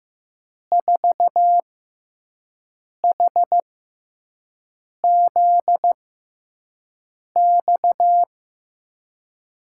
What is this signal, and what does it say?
Morse code, heard as a keyed tone, about 15 wpm, spaced as for 5 wpm, 700 Hz. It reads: 4HZX
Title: USA Call Signs - 6 Characters - 6x Character Spacing - 15wpm